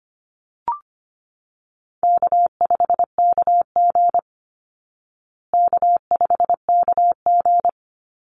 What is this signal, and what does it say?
Morse code, heard as a keyed tone, about 25 wpm, 700 Hz. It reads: K5KG K5KG
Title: Easy Call Signs Repeated 2x - 25wpm